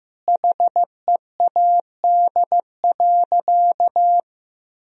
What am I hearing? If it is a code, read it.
Morse code, heard as a keyed tone, about 15 wpm, 700 Hz. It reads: HEAD.
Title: Sentences from Top 1000 Words - Review Part 3 - 15wpm